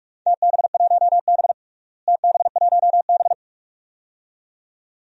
Morse code, heard as a keyed tone, about 41 wpm, 700 Hz. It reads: TB1B TB1B
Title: DXCC-100 Call Signs - 4 Characters - Repeated 2x - 45wpm